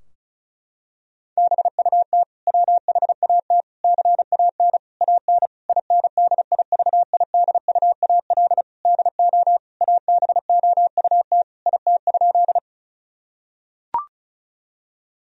Morse code, heard as a keyed tone, about 34 wpm, 700 Hz. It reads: BUT WHAT CAN AN INDIVIDUAL DO ABOUT IT?
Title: Sentences from Top 1000 Words - Review Part 4 - 35wpm